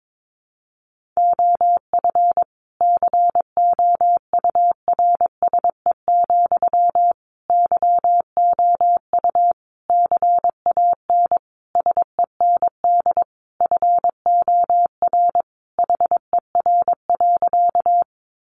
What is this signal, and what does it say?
Morse code, heard as a keyed tone, about 22 wpm, 700 Hz. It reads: OF COURSE, YOU CAN SEND FOR HER.
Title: Sentences from Top 1500 Words 22wpm